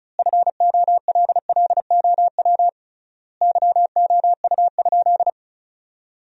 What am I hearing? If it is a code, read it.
Morse code, heard as a keyed tone, about 36 wpm, 700 Hz. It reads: FOLLOW YOU?
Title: 3x Word-Spacing - Sentences from Top 200 Words - 35wpm